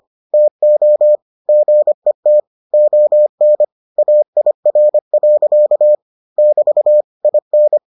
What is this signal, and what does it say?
Morse code, heard as a keyed tone, about 25 wpm, 600 Hz. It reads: TO GET ON AIR. <BT> IN